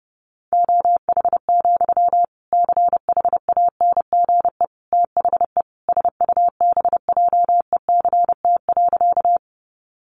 Morse code, heard as a keyed tone, about 30 wpm, 700 Hz. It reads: OH, CHANGE THE SUBJECT.
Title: Sentences from Top 800 Words 30wpm